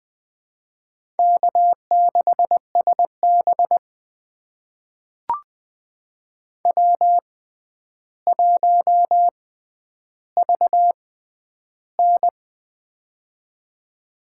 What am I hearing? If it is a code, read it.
Morse code, heard as a keyed tone, about 20 wpm, spaced as for 10 wpm, 700 Hz. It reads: K6SB W1VN
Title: USA Call Signs - 4 Characters - 6x Character Spacing - 20wpm